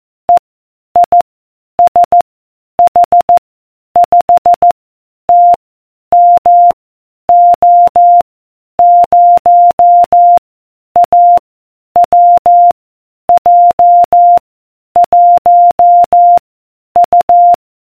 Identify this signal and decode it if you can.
Morse code, heard as a keyed tone, about 14 wpm, 700 Hz. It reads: E I S H 5 T M O 0 A W J 1 U